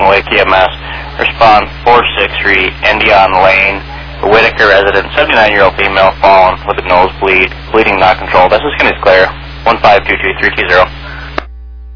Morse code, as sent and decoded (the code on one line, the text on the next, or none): none